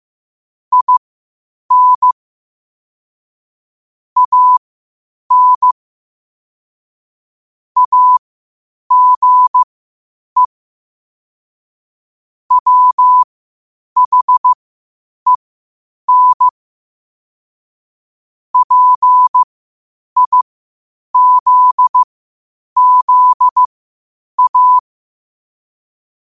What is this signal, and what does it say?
Morse code, heard as a keyed tone, about 15 wpm, spaced as for 8 wpm, 1000 Hz. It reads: IN AN AGE WHEN PIZZA
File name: qotdfiles / 8wpm